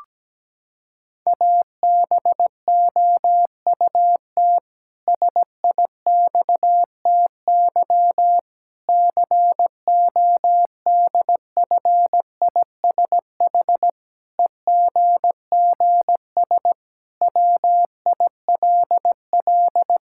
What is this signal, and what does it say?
Morse code, heard as a keyed tone, about 17 wpm, 700 Hz. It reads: ABOUT SIXTY CODFISH EGGS WILL
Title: Pangram Challenge v2 - 17wpm